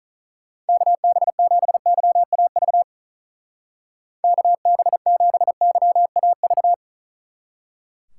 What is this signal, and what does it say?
Morse code, heard as a keyed tone, about 46 wpm, 700 Hz. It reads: KB7YAV KB7YAV